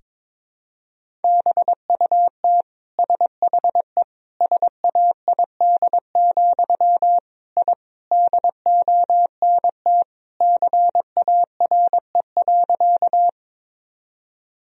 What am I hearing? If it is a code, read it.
Morse code, heard as a keyed tone, about 22 wpm, 700 Hz. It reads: BUT SHE SAID, I DONT CARE.
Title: Sentences from Top 1000 Words - Review Part 1 - 22wpm